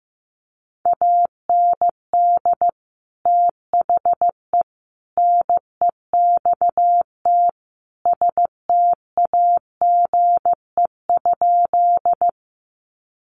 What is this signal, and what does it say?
Morse code, heard as a keyed tone, about 15 wpm, 700 Hz. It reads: AND THE NEXT STAGE?
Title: Sentences from Top 1200 Words 15wpm